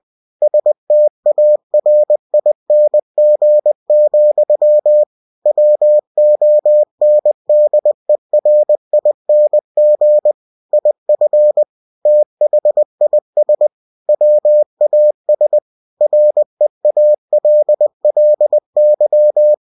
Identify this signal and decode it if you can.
Morse code, heard as a keyed tone, about 20 wpm, 600 Hz. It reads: STARING, WONDERING IF THIS WAS REALLY